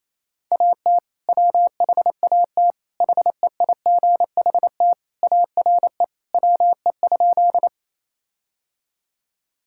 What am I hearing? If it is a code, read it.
Morse code, heard as a keyed tone, about 27 wpm, 700 Hz. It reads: AT WHAT HEIGHT ARE WE?
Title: Sentences from Top 1600 Words 28wpm